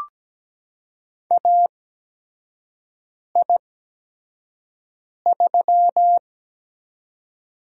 Morse code, heard as a keyed tone, about 17 wpm, spaced as for 5 wpm, 700 Hz. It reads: AI3
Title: USA Call Signs - 4 Characters - 8x Character Spacing - 17wpm